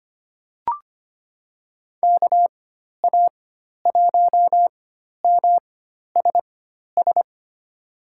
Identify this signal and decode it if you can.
Morse code, heard as a keyed tone, about 25 wpm, spaced as for 12 wpm, 700 Hz. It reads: KA1MSS
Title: USA Call Signs - 6 Characters - 4x Character Spacing - 25wpm